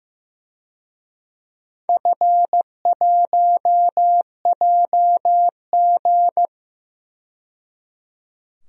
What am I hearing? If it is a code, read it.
Morse code, heard as a keyed tone, about 16 wpm, 700 Hz. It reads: F1JG
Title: DXCC-100 Call Signs - 4 Characters - 15wpm